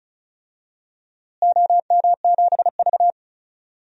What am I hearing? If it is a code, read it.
Morse code, heard as a keyed tone, about 36 wpm, 700 Hz. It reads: OM7V
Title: DXCC-100 Call Signs - 4 Characters - Repeated 2x - 35wpm